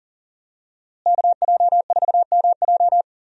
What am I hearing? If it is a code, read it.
Morse code, heard as a keyed tone, about 40 wpm, 700 Hz. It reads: KJ4MJ
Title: USA Call Signs - 5 Characters - Repeated 2x - 40wpm